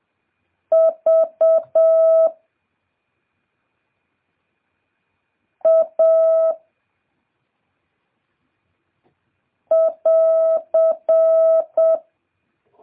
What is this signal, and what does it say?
Morse code, heard as a keyed tone, about 7 wpm, 650 Hz. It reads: V A <AR>